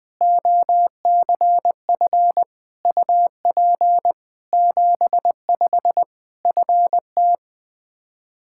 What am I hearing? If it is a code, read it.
Morse code, heard as a keyed tone, about 21 wpm, 700 Hz. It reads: OCF UP 75 FT
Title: K1USN SST QSO Practice 20wpm